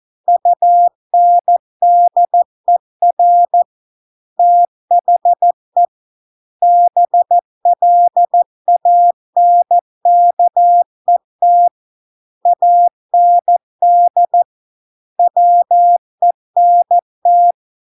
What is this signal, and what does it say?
Morse code, heard as a keyed tone, about 14 wpm, 700 Hz. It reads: UNDER THE BLANKET AND WENT